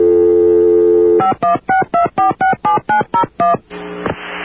DTMF, heard as telephone touch tones, tones of about 130 ms. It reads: 5162467901